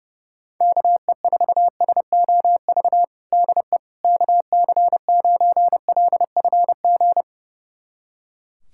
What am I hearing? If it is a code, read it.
Morse code, heard as a keyed tone, about 30 wpm, 700 Hz. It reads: KE4SOV DE KC9LFG
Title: USA Callsign DE Callsign - 6 Characters - 30wpm